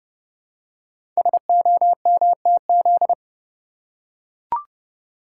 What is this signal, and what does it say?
Morse code, heard as a keyed tone, about 30 wpm, 700 Hz. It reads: SOMTZ
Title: Code Groups - 5 Letter:Numbers - 30wpm